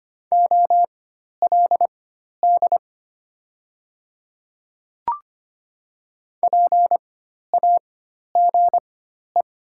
Morse code, heard as a keyed tone, about 25 wpm, spaced as for 12 wpm, 690 Hz. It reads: OLD PAGE
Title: 4x Character Spacing - Top 300 Words - 25wpm